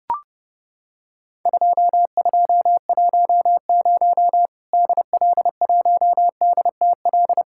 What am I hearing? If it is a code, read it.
Morse code, heard as a keyed tone, about 30 wpm, 700 Hz. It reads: 2210 DL1DTL
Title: Worked All Europe DX Contest - QTC Exchange - 30wpm